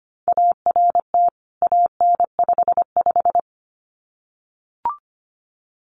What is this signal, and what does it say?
Morse code, heard as a keyed tone, about 25 wpm, 700 Hz. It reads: ART AN55
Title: CWT Exchange 25wpm